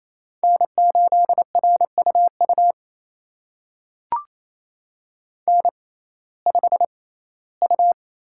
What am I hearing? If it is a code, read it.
Morse code, heard as a keyed tone, about 28 wpm, spaced as for 17 wpm, 700 Hz. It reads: N8RUU N5U
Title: USA Call Signs - 5 Characters - 6x Character Spacing - 28wpm